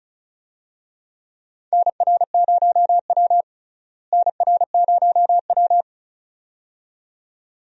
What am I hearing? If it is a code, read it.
Morse code, heard as a keyed tone, about 34 wpm, 700 Hz. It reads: NR0W NR0W
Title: USA Call Signs - 4 Characters - Repeated 2x - 35wpm